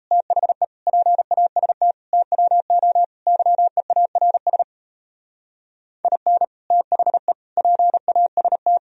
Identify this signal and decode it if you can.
Morse code, heard as a keyed tone, about 35 wpm, 700 Hz. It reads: THE PAST TWO YEARS IN THE PAST